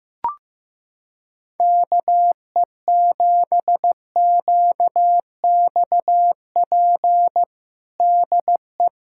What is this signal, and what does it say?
Morse code, heard as a keyed tone, about 15 wpm, 700 Hz. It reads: KE7QXP DE
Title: USA Callsign DE Callsign 15wpm